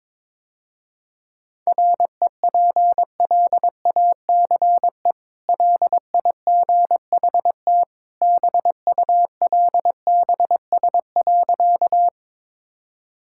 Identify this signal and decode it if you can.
Morse code, heard as a keyed tone, about 23 wpm, 700 Hz. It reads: REPLACE LIGHT BULBS.
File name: practice / 22wpm / Easy Sentences with Hints - 22wpm